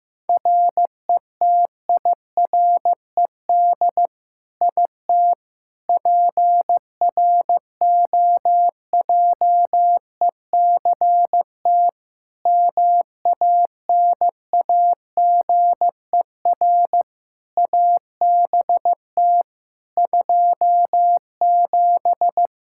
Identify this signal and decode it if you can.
Morse code, heard as a keyed tone, about 15 wpm, 700 Hz. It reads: RETIRED IT PROJECT MANAGER ABT 27
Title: Practice QSO - Exchange 3 - More Info - 15wpm